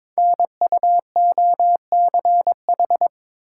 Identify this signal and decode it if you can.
Morse code, heard as a keyed tone, about 22 wpm, 700 Hz. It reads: NUOCH
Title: Code Groups - 5 Letters - 22wpm